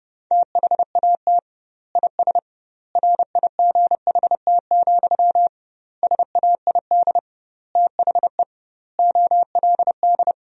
Morse code, heard as a keyed tone, about 32 wpm, 700 Hz. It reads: THAT IS RIGHT, SAID THE OLD
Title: 2x Word-Spacing - Sentences from Top 200 Words - 30wpm